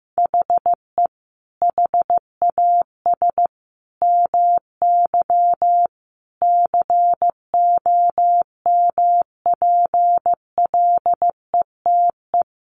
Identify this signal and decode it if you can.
Morse code, heard as a keyed tone, about 15 wpm, 700 Hz. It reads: HE HAS MY COMPLETE
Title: Sentences from Top 1500 Words 15wpm